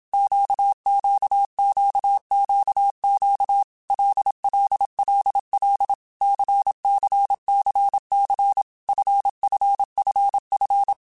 Morse code, heard as a keyed tone, about 27 wpm, 800 Hz. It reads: QQQQQ LLLL CCCC FFFF
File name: Basic Letter Groups By7Groups / LFCYQ / LFCYQ@25